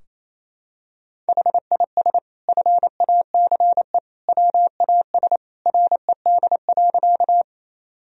Morse code, heard as a keyed tone, about 28 wpm, 700 Hz. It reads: HIS FACE WAS RED.
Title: Sentences from Top 1000 Words - Review Part 2 - 28wpm